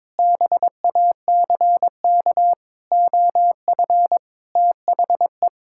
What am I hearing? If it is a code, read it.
Morse code, heard as a keyed tone, about 22 wpm, 700 Hz. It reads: BACK OF THE
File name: practice / 22wpm / Sentences with 8-Word Phrases - 22wpm